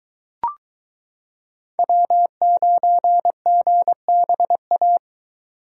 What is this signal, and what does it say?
Morse code, heard as a keyed tone, about 22 wpm, 700 Hz. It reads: W9GBA